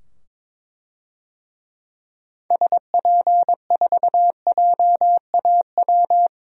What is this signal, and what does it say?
Morse code, heard as a keyed tone, about 22 wpm, 700 Hz. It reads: SP4JAW
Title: DXCC-100 Call Signs - 6 Characters - 22wpm